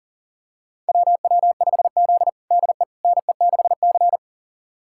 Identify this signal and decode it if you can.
Morse code, heard as a keyed tone, about 37 wpm, 700 Hz. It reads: WW5Z DE NE6C
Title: USA Callsign DE Callsign - 4 Characters - 40wpm